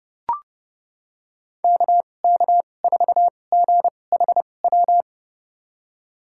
Morse code, heard as a keyed tone, about 31 wpm, spaced as for 22 wpm, 700 Hz. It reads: KK4GHW
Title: USA Call Signs - 6 Characters - 2x Character Spacing - 30wpm